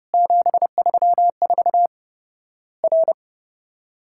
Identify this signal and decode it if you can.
Morse code, heard as a keyed tone, about 30 wpm, 700 Hz. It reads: 734 R
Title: Worked All Europe DX Contest - QTC Exchange - 30wpm